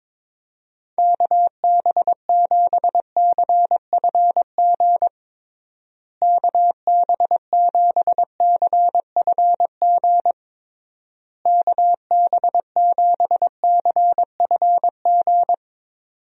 Morse code, heard as a keyed tone, about 22 wpm, 700 Hz. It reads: KB7CFG KB7CFG KB7CFG
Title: USA Call Signs - 6 Characters - Repeated 3x - 22wpm